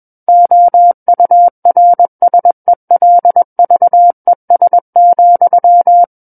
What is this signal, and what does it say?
Morse code, heard as a keyed tone, about 21 wpm, 700 Hz. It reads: OURSELVES,
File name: Book One / 12-Chapter 12 - 21 WPM